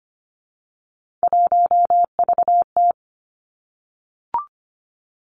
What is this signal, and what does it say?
Morse code, heard as a keyed tone, about 25 wpm, 700 Hz. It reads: 1VT
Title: Three Challenging Letter-Number Combinations 25wpm